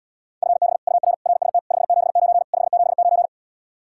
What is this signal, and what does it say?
Morse code, heard as a keyed tone, about 79 wpm, 700 Hz. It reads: FB UR RST 579 579